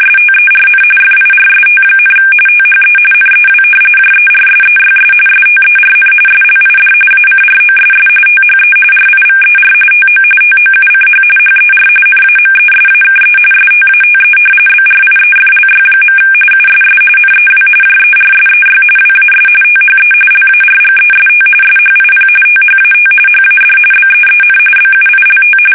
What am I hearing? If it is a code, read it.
RTTY, about 45.45 baud. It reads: STERN FIRST, THE NEW JERSEYS STERN SLID ACROSS THE RIVER TO TOUCH HER NAMESAKE STATE BRIEFLY. USS NEW JERSEY WAS THE LARGEST SHIP THAT HAD BEEN BUILT
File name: fsk850 / battleshipnj